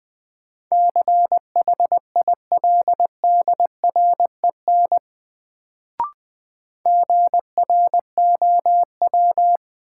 Morse code, heard as a keyed tone, about 20 wpm, 700 Hz. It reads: CHILDREN GROW
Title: Sentences from Top 300 Words - New Word Drill - 20wpm